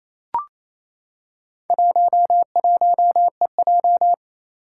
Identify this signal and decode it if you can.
Morse code, heard as a keyed tone, about 28 wpm, 700 Hz. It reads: 11EJ
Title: Four Challenging Letter-Number Combinations 28wpm